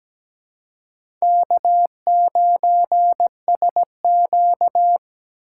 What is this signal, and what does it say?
Morse code, heard as a keyed tone, about 17 wpm, 700 Hz. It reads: K9SQ
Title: USA Call Signs - 4 Characters - 2x Character Spacing - 17wpm